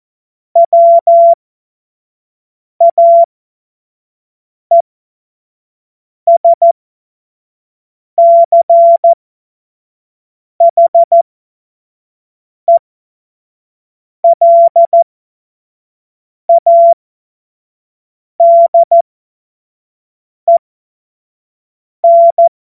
Morse code, heard as a keyed tone, about 14 wpm, spaced as for 5 wpm, 680 Hz. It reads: WAESCHELADEN
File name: Morsekurs / Witze / Witz 01. Februar 2022 25 F70